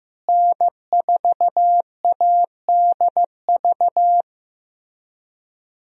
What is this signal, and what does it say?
Morse code, heard as a keyed tone, about 15 wpm, 700 Hz. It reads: N4ADV
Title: USA Call Signs - 5 Characters - Repeated 3x - 15wpm